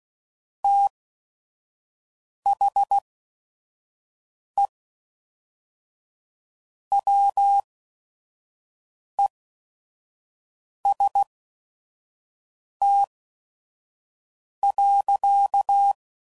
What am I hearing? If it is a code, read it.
Morse code, heard as a keyed tone, about 16 wpm, spaced as for 5 wpm, 800 Hz. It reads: THE WEST.
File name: CW Code Groups @5WPM / Weather Sets / Weather 5 1